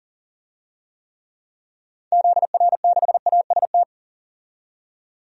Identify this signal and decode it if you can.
Morse code, heard as a keyed tone, about 40 wpm, 700 Hz. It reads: ZR6AST
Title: DXCC-100 Call Signs - 6 Characters - 40wpm